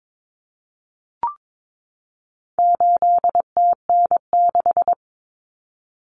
Morse code, heard as a keyed tone, about 22 wpm, 700 Hz. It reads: 8TN6
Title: Common Cut Numbers - 4-Digit 22wpm